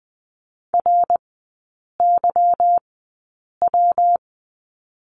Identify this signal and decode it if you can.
Morse code, heard as a keyed tone, about 20 wpm, spaced as for 8 wpm, 690 Hz. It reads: RYW
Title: 145 Lesson 24e - Letter y - Three Characters 20wpm